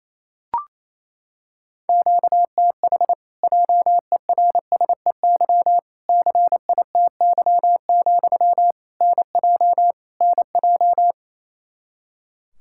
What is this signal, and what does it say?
Morse code, heard as a keyed tone, about 28 wpm, 700 Hz. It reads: QTH JERSEY CITY, NJ NJ
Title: Practice QSO - Exchange 1 - Beginning - 28wpm